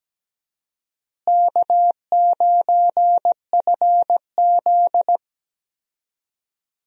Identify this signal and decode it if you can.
Morse code, heard as a keyed tone, about 17 wpm, 700 Hz. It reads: K9FZ
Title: USA Call Signs - 4 Characters - Repeated 2x - 17wpm